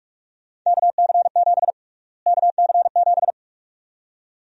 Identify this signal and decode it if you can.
Morse code, heard as a keyed tone, about 47 wpm, 700 Hz. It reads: KC7 KC7
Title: USA Call Sign Prefixes - Repeated 2x - 45wpm